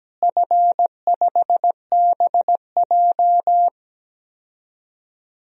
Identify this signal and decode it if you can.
Morse code, heard as a keyed tone, about 17 wpm, 700 Hz. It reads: F5BJ
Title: DXCC-100 Call Signs - 4 Characters - Repeated 2x - 17wpm